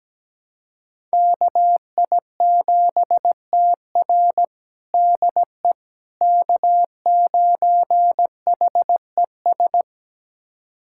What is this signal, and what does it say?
Morse code, heard as a keyed tone, about 17 wpm, 700 Hz. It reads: KI7TR DE K9HES
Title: USA Callsign DE Callsign - 5 Characters - 17wpm